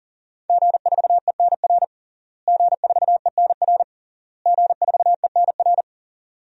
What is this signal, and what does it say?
Morse code, heard as a keyed tone, about 40 wpm, 700 Hz. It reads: G4ENR G4ENR G4ENR
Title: DXCC-100 Call Signs - 5 Characters - Repeated 3x - 40wpm